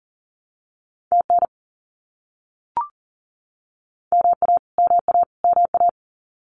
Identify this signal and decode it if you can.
Morse code, heard as a keyed tone, about 40 wpm, 700 Hz. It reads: TN MA MA MA
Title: US States Repeated 3x 40wpm